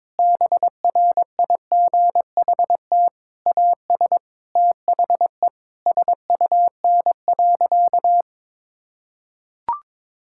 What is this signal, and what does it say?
Morse code, heard as a keyed tone, about 22 wpm, 700 Hz. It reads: BRIGHT AS THE SUN.
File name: Difficult Sentences with Hints - 22wpm